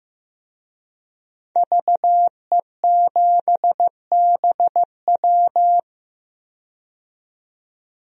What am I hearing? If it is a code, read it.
Morse code, heard as a keyed tone, about 15 wpm, 700 Hz. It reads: VE7BW
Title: DXCC-100 Call Signs - 5 Characters - 15wpm